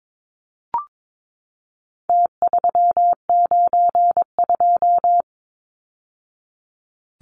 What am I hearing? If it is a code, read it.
Morse code, heard as a keyed tone, about 22 wpm, 700 Hz. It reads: T392
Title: Numbers - Contest Style - 4-Digit 22wpm